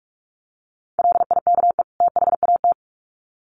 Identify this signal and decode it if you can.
Morse code, heard as a keyed tone, about 44 wpm, 700 Hz. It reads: LIKE THAT
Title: Sets of 2 Words 45wpm